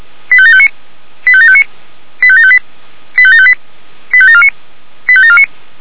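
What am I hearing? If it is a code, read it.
Five-tone selective call (ZVEI), heard as quick runs of five tones, tones of about 70 ms. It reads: 85759 85759 85758 85758 85749 85749
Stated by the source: Received on an SDR, ZVEI five-tone call 85730.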